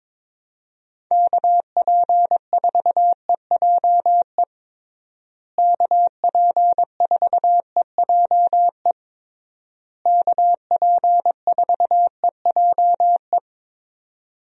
Morse code, heard as a keyed tone, about 22 wpm, 700 Hz. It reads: KP4EJE KP4EJE KP4EJE
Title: DXCC-100 Call Signs - 6 Characters - Repeated 3x - 22wpm